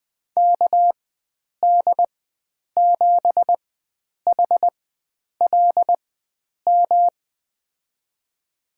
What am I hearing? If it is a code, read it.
Morse code, heard as a keyed tone, about 20 wpm, spaced as for 9 wpm, 700 Hz. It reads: KD7HLM
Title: USA Call Signs - 6 Characters - 4x Character Spacing - 20wpm